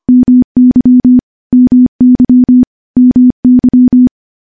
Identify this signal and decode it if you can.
Morse code, heard as a keyed tone, about 25 wpm, 250 Hz. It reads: MY MY MY
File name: morse / morse-training-25WPM